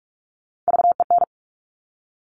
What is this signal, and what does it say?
Morse code, heard as a keyed tone, about 46 wpm, 700 Hz. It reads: VEN
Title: Top Three Letter Combinations in Words 45wpm